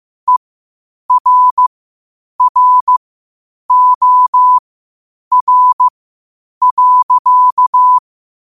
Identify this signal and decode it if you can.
Morse code, heard as a keyed tone, about 15 wpm, spaced as for 8 wpm, 1000 Hz. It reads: ERROR.
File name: qotdfiles / 8wpm